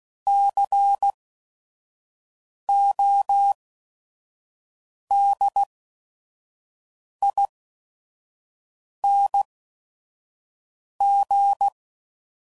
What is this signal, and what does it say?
Morse code, heard as a keyed tone, about 16 wpm, spaced as for 5 wpm, 800 Hz. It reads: CODING
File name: Suffixes / Special ING's / VerySlowToFaster ING's